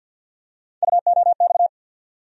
Morse code, heard as a keyed tone, about 47 wpm, 700 Hz. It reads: UOX
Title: USA Call Sign Postfixes - 3 Letters 50wpm